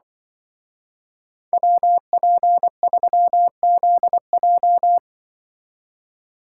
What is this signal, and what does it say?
Morse code, heard as a keyed tone, about 24 wpm, 700 Hz. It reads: WP3ZJ
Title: Speed Racing - USA 2x2 Call Signs 20wpm